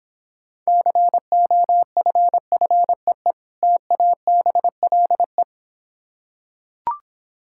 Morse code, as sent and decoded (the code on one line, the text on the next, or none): -.-. --- ..-. ..-. . / - .- -... .-.. .
COFFEE TABLE